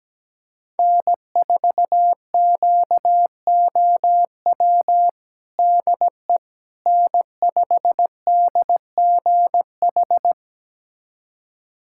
Morse code, heard as a keyed tone, about 17 wpm, 700 Hz. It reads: N4QOW DE N5DGH
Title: USA Callsign DE Callsign - 5 Characters - 17wpm